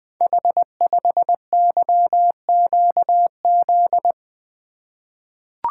Morse code, heard as a keyed tone, about 21 wpm, 700 Hz. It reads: H5YQZ
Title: Code Groups - 5 Letter:Numbers - 20wpm